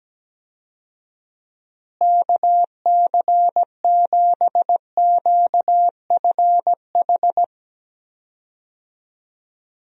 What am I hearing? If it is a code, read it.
Morse code, heard as a keyed tone, about 17 wpm, 700 Hz. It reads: KC7QFH
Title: USA Call Signs - 6 Characters - Repeated 3x - 17wpm